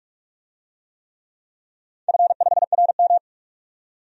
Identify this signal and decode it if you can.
Morse code, heard as a keyed tone, about 46 wpm, 700 Hz. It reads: F5RM
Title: DXCC-100 Call Signs - 4 Characters - 45wpm